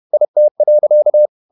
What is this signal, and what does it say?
Morse code, heard as a keyed tone, about 31 wpm, 600 Hz. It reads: IT.
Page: Never 31wpm | CW med Gnister